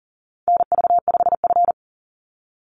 Morse code, heard as a keyed tone, about 40 wpm, 700 Hz. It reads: NV5F
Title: Call Signs - 40wpm